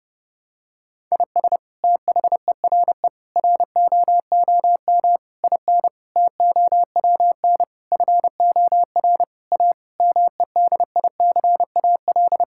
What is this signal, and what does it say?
Morse code, heard as a keyed tone, about 30 wpm, 700 Hz. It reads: IS THERE ROOM IN TOWN FOR A MEDICAL
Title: Sentences from Top 1000 Words - Review Part 5 - 30wpm